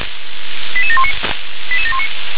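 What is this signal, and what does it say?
Five-tone selective call (ZVEI), heard as quick runs of five tones, tones of about 70 ms. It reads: 88719 88719
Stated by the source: ZVEI five-tone selective call on 357 received by SDR.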